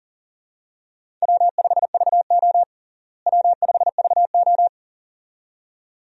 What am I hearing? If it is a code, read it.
Morse code, heard as a keyed tone, about 40 wpm, 700 Hz. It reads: W5VO W5VO
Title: USA Call Signs - 4 Characters - Repeated 2x - 40wpm